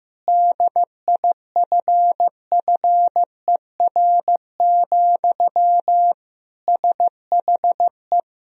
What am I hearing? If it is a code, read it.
Morse code, heard as a keyed tone, about 15 wpm, 700 Hz. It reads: DIFFER, SHE
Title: Sentences from Top 1000 Words - Review Part 1 - 15wpm